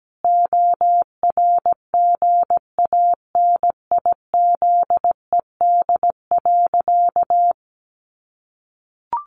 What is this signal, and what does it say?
Morse code, heard as a keyed tone, about 17 wpm, 700 Hz. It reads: ORGANIZED.
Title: Sentences from Top 1300 Words 17wpm